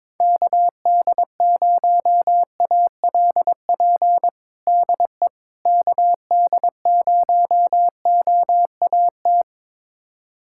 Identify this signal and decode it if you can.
Morse code, heard as a keyed tone, about 21 wpm, 700 Hz. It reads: KD0ALP DE KD0OAT
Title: USA Callsign DE Callsign 22wpm